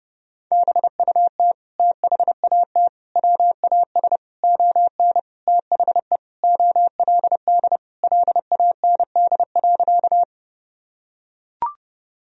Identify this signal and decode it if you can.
Morse code, heard as a keyed tone, about 32 wpm, 700 Hz. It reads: BUT THAT WAS ON THE OLD LAND.
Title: Sentences from Top 1000 Words - Review Part 1 - 30wpm